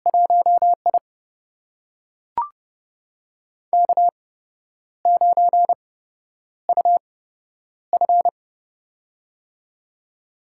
Morse code, heard as a keyed tone, about 30 wpm, spaced as for 10 wpm, 700 Hz. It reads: 1I K9UF
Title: USA Call Signs - 4 Characters - 8x Character Spacing - 30wpm